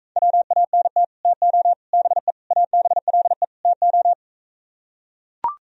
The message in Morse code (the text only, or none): WANT TO BE ABLE TO